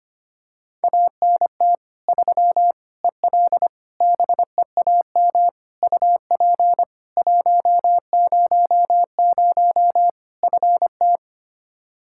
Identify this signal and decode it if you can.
Morse code, heard as a keyed tone, about 25 wpm, 700 Hz. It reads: ANT 3 EL BEAM UP 100 FT
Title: Practice QSO - Exchange 2 - Info - 25wpm